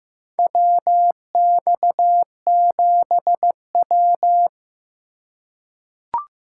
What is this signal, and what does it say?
Morse code, heard as a keyed tone, about 15 wpm, 700 Hz. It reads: WX7W
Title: USA Call Signs - 4 Characters - 2x Character Spacing - 15wpm